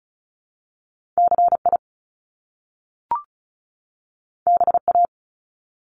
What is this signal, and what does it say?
Morse code, heard as a keyed tone, about 36 wpm, 700 Hz. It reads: CI BA
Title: Top Two Letter Combinations in Words 35wpm